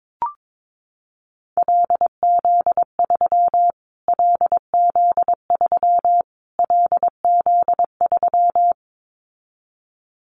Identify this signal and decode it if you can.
Morse code, heard as a keyed tone, about 22 wpm, 700 Hz. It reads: LZ3 LZ3 LZ3
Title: Callsign Prefixes for Top 25 DXCC - Repeated 3x - 22wpm